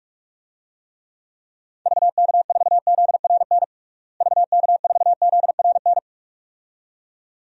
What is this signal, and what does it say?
Morse code, heard as a keyed tone, about 46 wpm, 700 Hz. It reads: VK4ZRN VK4ZRN
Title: DXCC-100 Call Signs - 6 Characters - Repeated 2x - 45wpm